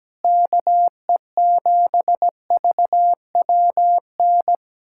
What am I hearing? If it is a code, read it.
Morse code, heard as a keyed tone, about 17 wpm, 700 Hz. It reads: KE7VWN
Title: USA Callsign DE Callsign - 6 Characters - 17wpm